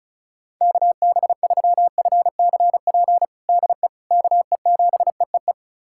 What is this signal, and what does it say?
Morse code, heard as a keyed tone, about 36 wpm, 700 Hz. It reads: KB3FCP DE KE7EEE
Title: USA Callsign DE Callsign 35wpm